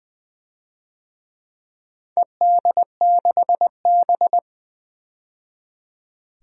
Morse code, heard as a keyed tone, about 20 wpm, 700 Hz. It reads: ED6B
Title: DXCC-100 Call Signs - 4 Characters - 20wpm